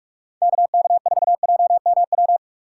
Morse code, heard as a keyed tone, about 46 wpm, 700 Hz. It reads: KK4JMW
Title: USA Call Signs - 6 Characters - 6x Character Spacing - 45wpm